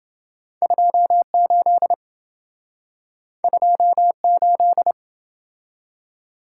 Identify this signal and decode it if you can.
Morse code, heard as a keyed tone, about 27 wpm, 700 Hz. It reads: 28 28